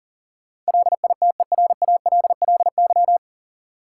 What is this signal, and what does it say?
Morse code, heard as a keyed tone, about 40 wpm, 700 Hz. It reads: LITERALLY
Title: Sentences from Top 1400 Words - New Word Drill Alt - 40wpm